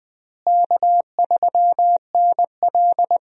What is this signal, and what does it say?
Morse code, heard as a keyed tone, about 21 wpm, 700 Hz. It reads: K3NL
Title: USA Call Signs - 4 Characters - Repeated 3x - 20wpm